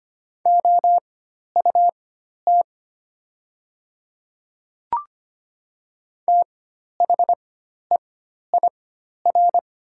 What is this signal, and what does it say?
Morse code, heard as a keyed tone, about 25 wpm, spaced as for 12 wpm, 700 Hz. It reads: OUT THEIR
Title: 4x Character Spacing - Top 100 Words - 25wpm